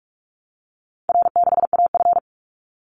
Morse code, heard as a keyed tone, about 44 wpm, 700 Hz. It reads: R6AF
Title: Call Signs Repeated 2x - 45wpm